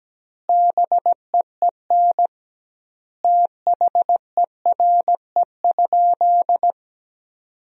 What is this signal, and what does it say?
Morse code, heard as a keyed tone, about 17 wpm, 700 Hz. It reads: BEEN THERE?
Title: 2x Word-Spacing - Sentences from Top 100 Words - 17wpm